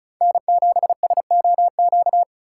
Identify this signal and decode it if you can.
Morse code, heard as a keyed tone, about 36 wpm, 700 Hz. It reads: N7SOQ
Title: USA Call Signs - 4x Character Spacing - 35wpm